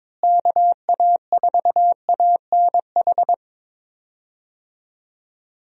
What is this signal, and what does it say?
Morse code, heard as a keyed tone, about 22 wpm, 700 Hz. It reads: KA4ANH
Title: USA Call Signs - 6 Characters - Repeated 2x - 22wpm